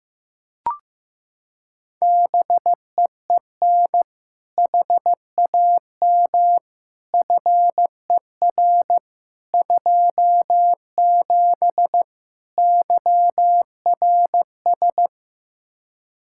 Morse code, heard as a keyed tone, about 15 wpm, 700 Hz. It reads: BEEN HAM FER 27 YRS
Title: K1USN SST QSO Practice 15wpm